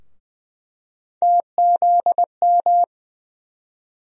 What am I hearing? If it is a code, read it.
Morse code, heard as a keyed tone, about 20 wpm, 700 Hz. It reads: TZM
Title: USA Call Sign Postfixes - 3 Letters 20wpm